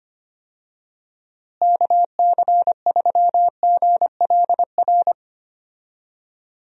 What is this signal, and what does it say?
Morse code, heard as a keyed tone, about 25 wpm, 700 Hz. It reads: KC3GLR
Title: USA Call Signs - 6 Characters - Repeated 3x - 25wpm